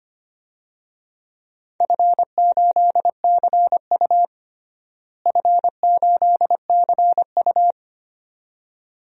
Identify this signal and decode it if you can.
Morse code, heard as a keyed tone, about 25 wpm, 700 Hz. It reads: F8CU F8CU
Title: DXCC-100 Call Signs - 4 Characters - Repeated 2x - 25wpm